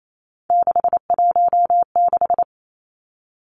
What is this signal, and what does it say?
Morse code, heard as a keyed tone, about 28 wpm, 700 Hz. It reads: 616
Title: Three Challenging Letter-Number Combinations 28wpm